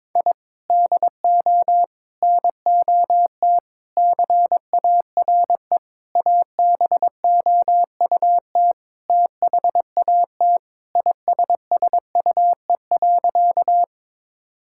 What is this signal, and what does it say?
Morse code, heard as a keyed tone, about 21 wpm, 700 Hz. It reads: I DO NOT CARE ABOUT THAT ISSUE.
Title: Sentences from Top 1000 Words - Review Part 6 - 22wpm